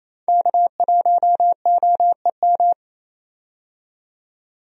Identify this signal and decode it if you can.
Morse code, heard as a keyed tone, about 28 wpm, 700 Hz. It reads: K1OEM
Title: USA Call Signs - 5 Characters - 28wpm